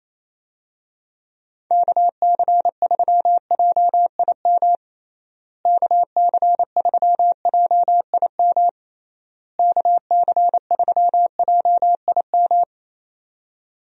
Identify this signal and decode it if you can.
Morse code, heard as a keyed tone, about 28 wpm, 700 Hz. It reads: KC3JIM KC3JIM KC3JIM
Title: USA Call Signs - 6 Characters - Repeated 3x - 28wpm